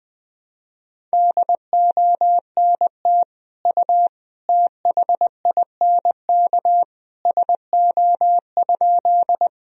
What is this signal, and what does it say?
Morse code, heard as a keyed tone, about 21 wpm, 700 Hz. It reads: DONT U THINK SO?
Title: Sentences from Top 300 Words and Abbreviations 20wpm